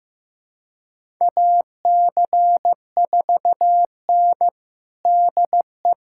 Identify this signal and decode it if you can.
Morse code, heard as a keyed tone, about 15 wpm, 700 Hz. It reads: AC4N DE
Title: USA Callsign DE Callsign - 4 Characters - 15wpm